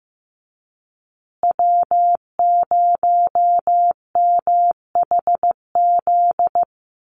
Morse code, heard as a keyed tone, about 15 wpm, 700 Hz. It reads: W0MHZ
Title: USA Call Signs Repeated 3x 15wpm